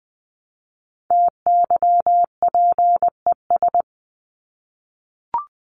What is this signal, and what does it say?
Morse code, heard as a keyed tone, about 20 wpm, 700 Hz. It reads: TYPES
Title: ES:S Suffix - Top 100 Words with Base Words 20wpm